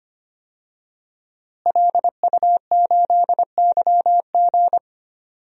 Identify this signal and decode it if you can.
Morse code, heard as a keyed tone, about 25 wpm, 700 Hz. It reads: LU8YG
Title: DXCC-100 Call Signs - 5 Characters - Repeated 3x - 25wpm